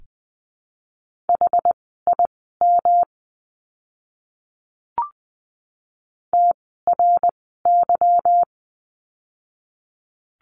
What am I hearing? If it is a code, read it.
Morse code, heard as a keyed tone, about 20 wpm, spaced as for 15 wpm, 700 Hz. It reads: HIM TRY
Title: 160 Lesson 26f - Letter g - Words 20wpm